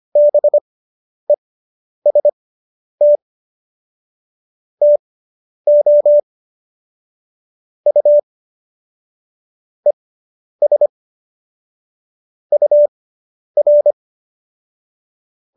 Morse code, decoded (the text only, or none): BEST TO U ES UR